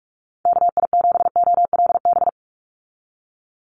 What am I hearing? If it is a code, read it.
Morse code, heard as a keyed tone, about 44 wpm, 700 Hz. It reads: KI7OLB
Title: USA Call Signs 45wpm